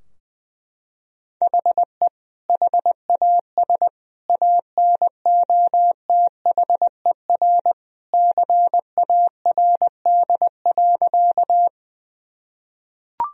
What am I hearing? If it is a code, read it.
Morse code, heard as a keyed tone, about 20 wpm, 700 Hz. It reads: HE HAS ANOTHER CARD.
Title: Sentences from Top 1000 Words - Review Part 4 - 20wpm